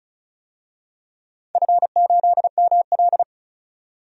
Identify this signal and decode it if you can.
Morse code, heard as a keyed tone, about 34 wpm, 700 Hz. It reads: F8ML
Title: DXCC-100 Call Signs - 4 Characters - 35wpm